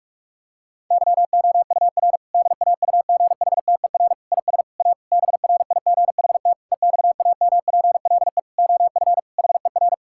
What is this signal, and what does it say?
Morse code, heard as a keyed tone, about 46 wpm, 700 Hz. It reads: YOUR DAUGHTER IS A BRIGHT EXAMPLE OF HER
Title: Sentences from Top 1600 Words 45wpm